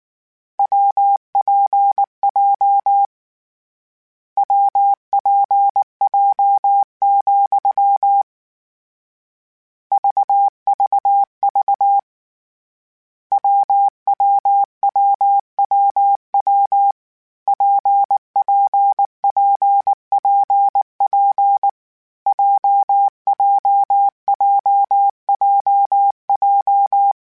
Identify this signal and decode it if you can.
Morse code, heard as a keyed tone, about 19 wpm, 800 Hz. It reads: WPJ WPJ, VVV WWWWW PPPPP JJJJJ
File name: Difficult Combos / Difficult Combos-mp3s / WPJ WPJ1 @18